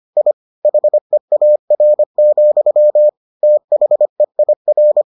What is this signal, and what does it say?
Morse code, heard as a keyed tone, about 25 wpm, 600 Hz. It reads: I HEAR, THEIR